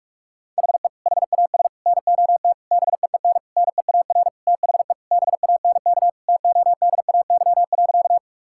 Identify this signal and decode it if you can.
Morse code, heard as a keyed tone, about 46 wpm, 700 Hz. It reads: HE HAS NOT BEEN NEAR THE BANK TODAY.
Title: Sentences from Top 1000 Words - Review Part 6 - 45wpm